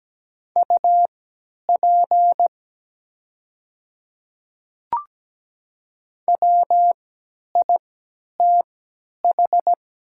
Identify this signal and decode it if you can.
Morse code, heard as a keyed tone, about 17 wpm, spaced as for 10 wpm, 700 Hz. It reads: UP WITH
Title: 3x Character Spacing - Top 100 Words - 17wpm